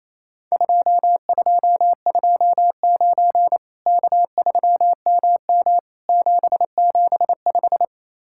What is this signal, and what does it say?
Morse code, heard as a keyed tone, about 28 wpm, 700 Hz. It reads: 2229 K3MM 775
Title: Worked All Europe DX Contest - QTC Exchange - 28wpm